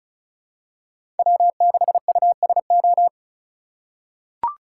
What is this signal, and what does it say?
Morse code, heard as a keyed tone, about 36 wpm, 700 Hz. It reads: W6USO
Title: USA Call Signs - 5 Characters - 3x Character Spacing - 35wpm